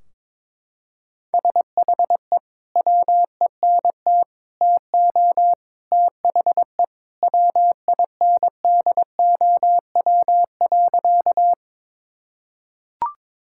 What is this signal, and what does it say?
Morse code, heard as a keyed tone, about 22 wpm, 700 Hz. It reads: SHE WENT TO THE WINDOW.
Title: Sentences from Top 1600 Words 22wpm